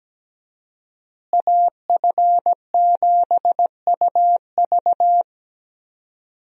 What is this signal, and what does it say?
Morse code, heard as a keyed tone, about 17 wpm, 700 Hz. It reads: AF7UV
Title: USA Call Signs - 5 Characters - Repeated 2x - 17wpm